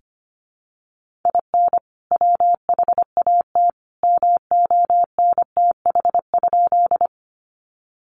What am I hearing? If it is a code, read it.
Morse code, heard as a keyed tone, about 25 wpm, 700 Hz. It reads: IN WHAT MONTH?
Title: Sentences from Top 800 Words 25wpm